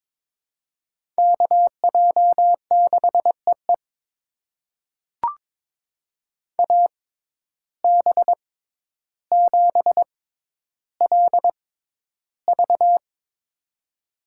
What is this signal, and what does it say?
Morse code, heard as a keyed tone, about 22 wpm, spaced as for 11 wpm, 700 Hz. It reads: KJ6EE AB7LV
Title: USA Call Signs - 5 Characters - 6x Character Spacing - 22wpm